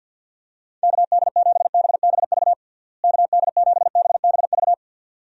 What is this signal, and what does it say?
Morse code, heard as a keyed tone, about 51 wpm, 700 Hz. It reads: KD7BBV KD7BBV
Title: USA Call Signs - 6 Characters - Repeated 2x - 50wpm